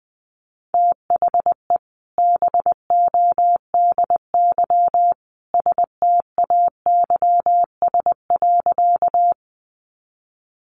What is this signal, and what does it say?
Morse code, heard as a keyed tone, about 20 wpm, 700 Hz. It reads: THE BODY STAYS.
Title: A Sound Of Thunder 20wpm